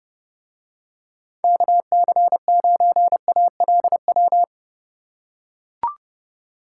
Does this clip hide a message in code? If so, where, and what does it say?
Yes: Morse code from 1.4 to 4.4 s, heard as a keyed tone, about 30 wpm, 700 Hz. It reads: KC9ALW